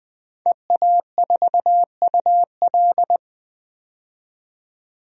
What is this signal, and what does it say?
Morse code, heard as a keyed tone, about 20 wpm, 700 Hz. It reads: EA4UL
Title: DXCC-100 Call Signs - 5 Characters - 20wpm